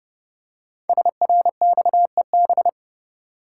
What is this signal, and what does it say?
Morse code, heard as a keyed tone, about 28 wpm, 700 Hz. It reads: SRXEB
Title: Code Groups - 5 Letters - 30wpm